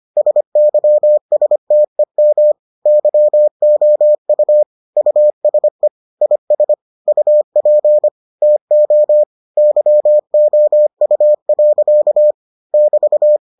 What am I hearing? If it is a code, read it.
Morse code, heard as a keyed tone, about 25 wpm, 600 Hz. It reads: SYSTEM YOU USE IS UP TO YOU. <BT>